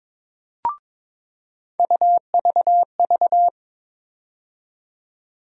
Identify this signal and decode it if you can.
Morse code, heard as a keyed tone, about 22 wpm, 700 Hz. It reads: UVV
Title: USA Call Sign Postfixes - 3 Letters 22wpm